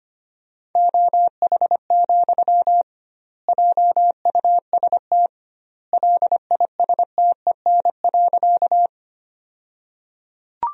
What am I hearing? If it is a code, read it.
Morse code, heard as a keyed tone, about 25 wpm, 700 Hz. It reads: OH, JUST LISTEN.
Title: 2x Word-Spacing - Sentences from Top 500 Words - 25wpm